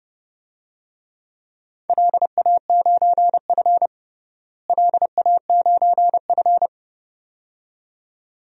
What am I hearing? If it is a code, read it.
Morse code, heard as a keyed tone, about 30 wpm, 700 Hz. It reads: LA9F LA9F
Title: DXCC-100 Call Signs - 4 Characters - Repeated 2x - 30wpm